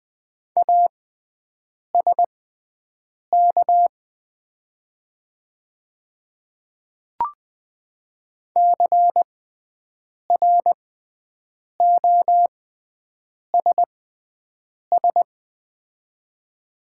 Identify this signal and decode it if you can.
Morse code, heard as a keyed tone, about 20 wpm, spaced as for 7 wpm, 700 Hz. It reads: ASK CROSS
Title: 6x Character Spacing - Top 300 Words - 20wpm